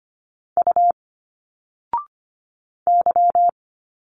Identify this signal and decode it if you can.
Morse code, heard as a keyed tone, about 25 wpm, 700 Hz. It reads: U Y
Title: Single Letters 25wpm